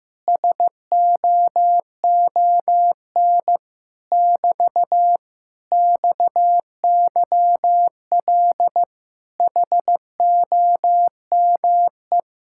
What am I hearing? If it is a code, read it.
Morse code, heard as a keyed tone, about 15 wpm, 700 Hz. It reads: SOON <BT> XYL HOME
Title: Speed-Racing Common QSO Phrases Encore - 15wpm